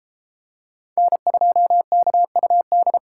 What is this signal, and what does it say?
Morse code, heard as a keyed tone, about 33 wpm, 700 Hz. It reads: N2KUD